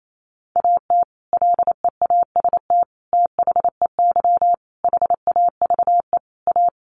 Morse code, heard as a keyed tone, about 28 wpm, 700 Hz. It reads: AT LEAST THEY HAVE A
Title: Sentences from Top 1500 Words 28wpm